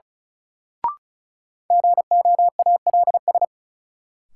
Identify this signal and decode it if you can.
Morse code, heard as a keyed tone, about 36 wpm, 700 Hz. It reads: GOALS
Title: Sentences from Top 1000 Words - Review Part 6 - New Word Drill - 35wpm